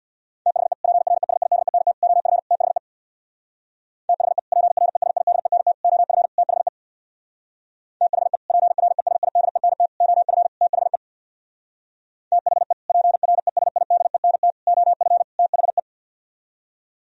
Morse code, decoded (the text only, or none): THE PRESIDENT OF THE THE PRESIDENT OF THE THE PRESIDENT OF THE THE PRESIDENT OF THE